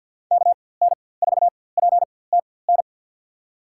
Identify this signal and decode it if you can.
Morse code, heard as a keyed tone, about 51 wpm, spaced as for 23 wpm, 700 Hz. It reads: KN4PTN
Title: USA Call Signs - 6 Characters - 4x Character Spacing - 50wpm